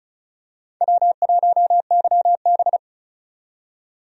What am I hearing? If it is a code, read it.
Morse code, heard as a keyed tone, about 34 wpm, 700 Hz. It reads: W1YB
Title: USA Call Signs - 4 Characters - 35wpm